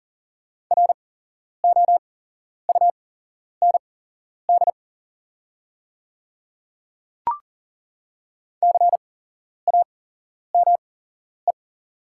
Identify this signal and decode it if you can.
Morse code, heard as a keyed tone, about 37 wpm, spaced as for 11 wpm, 690 Hz. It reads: ROUND CAME
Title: 8x Character Spacing - Top 500 Words - 40wpm